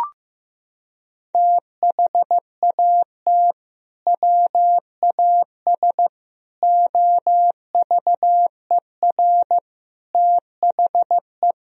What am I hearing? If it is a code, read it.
Morse code, heard as a keyed tone, about 15 wpm, 700 Hz. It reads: THAT WAS OVER THE
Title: Sentences from Top 1000 Words - Review Part 2 - 15wpm